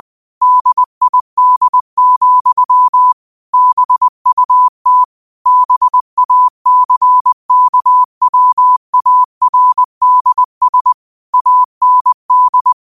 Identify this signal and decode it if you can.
Morse code, heard as a keyed tone, about 20 wpm, 1000 Hz. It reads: DID, BUT BACKWARDS AND
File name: qotdfiles / 20wpm